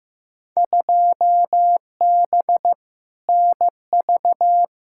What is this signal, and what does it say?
Morse code, heard as a keyed tone, about 16 wpm, 700 Hz. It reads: 2B NV
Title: Field Day Exchanges Encore - 15wpm